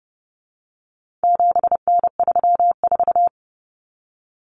Morse code, heard as a keyed tone, about 30 wpm, 700 Hz. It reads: 7N34
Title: Common Cut Numbers - 4-Digit 30wpm